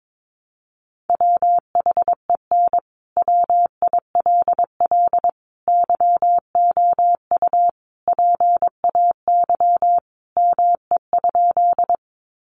Morse code, heard as a keyed tone, about 22 wpm, 700 Hz. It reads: WHEN WILL YOU PAY ME?
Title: Sentences from Top 700 Words 22wpm